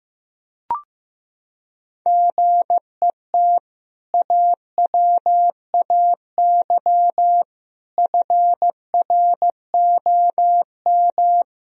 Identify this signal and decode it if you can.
Morse code, heard as a keyed tone, about 15 wpm, 700 Hz. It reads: GET AWAY FROM
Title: Sentences from Top 1000 Words - Review Part 2 - 15wpm